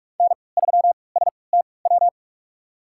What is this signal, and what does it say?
Morse code, heard as a keyed tone, about 46 wpm, spaced as for 25 wpm, 700 Hz. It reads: N3STW
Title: USA Call Signs - 5 Characters - 3x Character Spacing - 45wpm